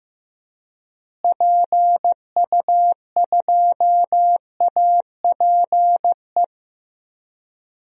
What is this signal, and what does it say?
Morse code, heard as a keyed tone, about 15 wpm, 700 Hz. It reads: PU2APE
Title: DXCC-100 Call Signs - 6 Characters - Repeated 3x - 15wpm